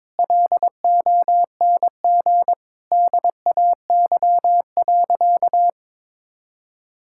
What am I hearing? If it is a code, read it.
Morse code, heard as a keyed tone, about 22 wpm, 700 Hz. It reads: LONG DAY.
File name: practice / 22wpm / Sentences with 3-Word Phrases - 22wpm